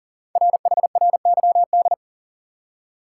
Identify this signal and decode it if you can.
Morse code, heard as a keyed tone, about 39 wpm, 700 Hz. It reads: RHRYD
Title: Code Groups - 5 Letters - 40wpm